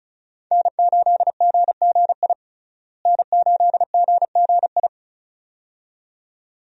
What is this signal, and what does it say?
Morse code, heard as a keyed tone, about 36 wpm, 700 Hz. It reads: N8GGI N8GGI
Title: USA Call Signs - 5 Characters - Repeated 2x - 35wpm